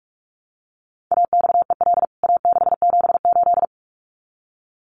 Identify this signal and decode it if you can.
Morse code, heard as a keyed tone, about 44 wpm, 700 Hz. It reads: AXEL A678
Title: CWT Exchange 45wpm